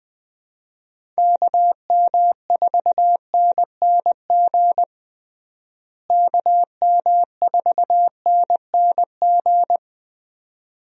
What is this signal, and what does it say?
Morse code, heard as a keyed tone, about 20 wpm, 700 Hz. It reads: KM4NNG KM4NNG
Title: USA Call Signs - 6 Characters - Repeated 2x - 20wpm